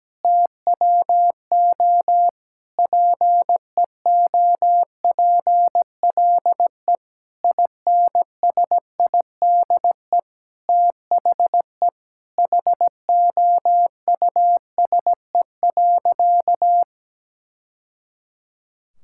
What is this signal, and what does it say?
Morse code, heard as a keyed tone, about 17 wpm, 700 Hz. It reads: TWO PEOPLE INSIDE THE HOUSE.
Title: Sentences from Top 1000 Words - Review Part 6 - 17wpm